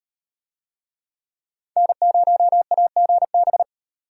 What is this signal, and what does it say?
Morse code, heard as a keyed tone, about 36 wpm, 700 Hz. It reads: N0AGB